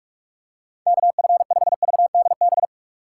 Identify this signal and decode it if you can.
Morse code, heard as a keyed tone, about 44 wpm, 700 Hz. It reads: KF5VDB